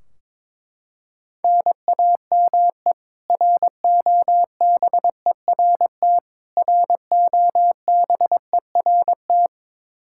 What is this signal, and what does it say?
Morse code, heard as a keyed tone, about 21 wpm, 700 Hz. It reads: NAME ROBERT ROBERT
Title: Practice QSO - Exchange 1 - Beginning - 22wpm